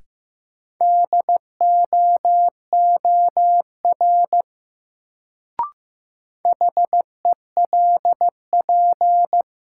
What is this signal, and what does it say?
Morse code, heard as a keyed tone, about 15 wpm, 700 Hz. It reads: DOOR HELP
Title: Sets of 2 Words - New Word Drill - 15wpm